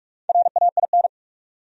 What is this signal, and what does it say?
Morse code, heard as a keyed tone, about 59 wpm, 700 Hz. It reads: RAIN